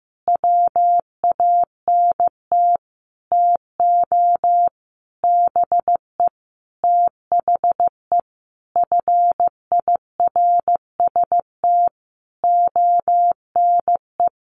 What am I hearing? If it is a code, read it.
Morse code, heard as a keyed tone, about 15 wpm, 700 Hz. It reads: WANT TO BE THE FIRST ONE
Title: Sentences from Top 1300 Words 15wpm